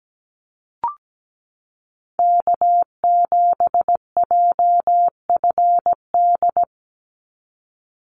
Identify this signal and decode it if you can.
Morse code, heard as a keyed tone, about 17 wpm, 700 Hz. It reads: K7JFD
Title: NA SOTA Chaser Callsigns 17wpm